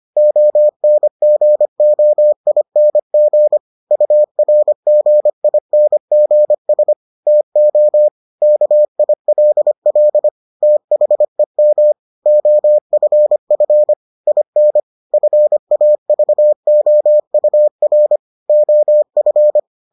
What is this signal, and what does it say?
Morse code, heard as a keyed tone, about 25 wpm, 600 Hz. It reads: ONGOING URGINGS TO KILL THEM OFF IN FAVOUR OF